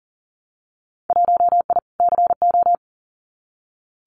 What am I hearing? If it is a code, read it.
Morse code, heard as a keyed tone, about 38 wpm, 700 Hz. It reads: 1I CO
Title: Winter Field Day Exchanges Encore 40wpm